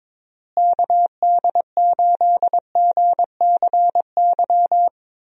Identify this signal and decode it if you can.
Morse code, heard as a keyed tone, about 22 wpm, 700 Hz. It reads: KD8GCY
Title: USA Call Signs - 6 Characters - Repeated 3x - 22wpm